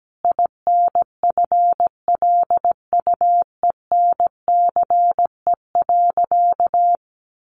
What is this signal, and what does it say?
Morse code, heard as a keyed tone, about 17 wpm, 700 Hz. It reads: INFLUENCE.
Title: Sentences from Top 1100 Words 17wpm